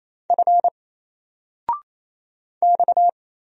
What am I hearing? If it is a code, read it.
Morse code, heard as a keyed tone, about 28 wpm, 700 Hz. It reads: F X
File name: Single-Letter-Number-28wpm